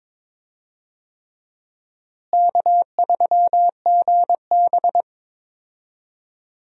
Morse code, heard as a keyed tone, about 22 wpm, 700 Hz. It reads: K3GB
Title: USA Call Signs - 4 Characters - Repeated 3x - 22wpm